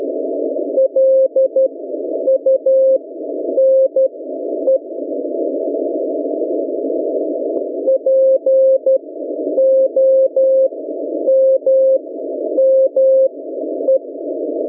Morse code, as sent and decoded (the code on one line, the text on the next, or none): .-.. ..- -. / .--. --- -- -- .
LUNE POMME